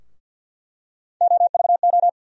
Morse code, heard as a keyed tone, about 49 wpm, 700 Hz. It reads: OVO
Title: USA Call Sign Postfixes - 3 Letters 50wpm